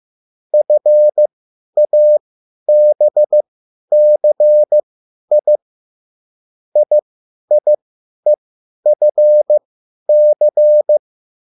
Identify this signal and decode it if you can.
Morse code, heard as a keyed tone, about 15 wpm, spaced as for 10 wpm, 600 Hz. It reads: FABCI IIEFC
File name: QSL / CW lesson